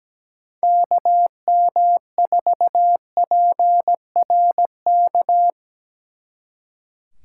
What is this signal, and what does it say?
Morse code, heard as a keyed tone, about 18 wpm, 700 Hz. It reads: KM4PRK
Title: Speed Racing - USA Call Signs 17wpm